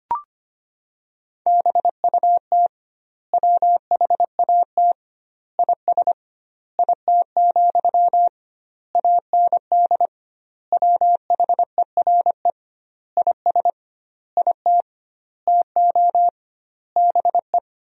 Morse code, heard as a keyed tone, about 25 wpm, 700 Hz. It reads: BUT WHAT IS IT, AND WHERE IS IT TO BE
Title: 2x Word-Spacing - Sentences from Top 300 Words - 25wpm